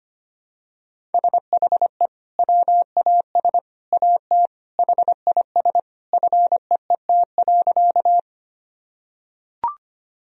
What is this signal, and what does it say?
Morse code, heard as a keyed tone, about 25 wpm, 700 Hz. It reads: SHE WAS AT HIS FEET.
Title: Sentences from Top 1000 Words - Review Part 1 - 25wpm